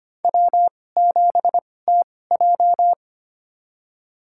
Morse code, heard as a keyed tone, about 25 wpm, spaced as for 18 wpm, 700 Hz. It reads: W7TJ
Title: USA Call Signs - 4 Characters - 2x Character Spacing - 25wpm